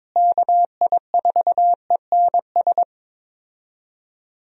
Morse code, heard as a keyed tone, about 22 wpm, 700 Hz. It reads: KI4ENS
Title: USA Call Signs - 6 Characters - Repeated 3x - 22wpm